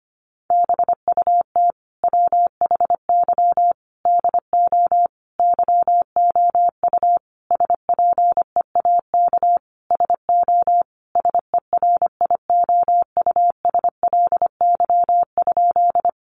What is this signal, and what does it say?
Morse code, heard as a keyed tone, about 25 wpm, 700 Hz. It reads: BUT WHY DO YOU SPEAK SO SERIOUSLY?
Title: Sentences from Top 1200 Words 25wpm